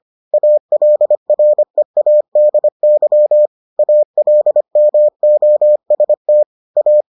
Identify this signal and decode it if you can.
Morse code, heard as a keyed tone, about 25 wpm, 600 Hz. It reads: ALREADY ALMOST A